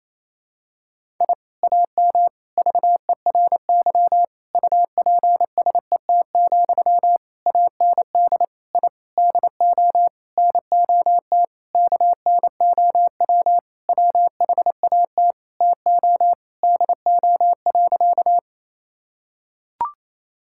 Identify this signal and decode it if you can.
Morse code, heard as a keyed tone, about 28 wpm, 700 Hz. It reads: I AM VERY UPSET, AND I DO NOT KNOW WHAT TO DO.
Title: Sentences from Top 1000 Words - Review Part 5 - 28wpm